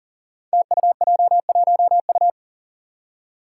Morse code, heard as a keyed tone, about 42 wpm, 700 Hz. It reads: TUJ1U
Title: Five Challenging Letter-Number Combinations 40wpm